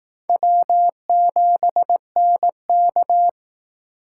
Morse code, heard as a keyed tone, about 18 wpm, 700 Hz. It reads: W7NK
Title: Speed Racing - USA Call Signs 15wpm